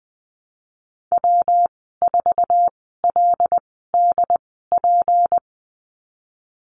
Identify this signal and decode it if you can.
Morse code, heard as a keyed tone, about 20 wpm, spaced as for 14 wpm, 700 Hz. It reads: W4LDP
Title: 154 Lesson 25g - Letter p - Call Signs 20wpm